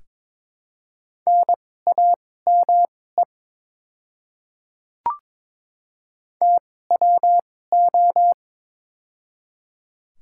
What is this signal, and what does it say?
Morse code, heard as a keyed tone, about 22 wpm, spaced as for 16 wpm, 700 Hz. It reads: NAME TWO
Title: 2x Character Spacing - Top 200 Words - 22wpm